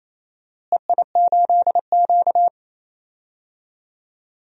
Morse code, heard as a keyed tone, about 28 wpm, 700 Hz. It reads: EI8Q
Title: DXCC-100 Call Signs - 4 Characters - Repeated 2x - 28wpm